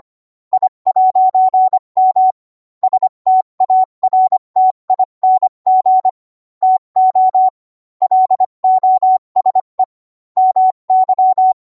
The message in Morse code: I'M STARTING TO LOSE MY